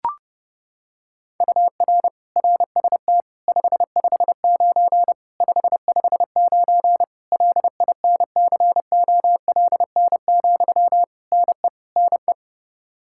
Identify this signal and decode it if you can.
Morse code, heard as a keyed tone, about 30 wpm, 700 Hz. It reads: UR RST 559 559 LINCOLN, NE NE
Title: Practice QSO - Exchange 1 - Beginning - 30wpm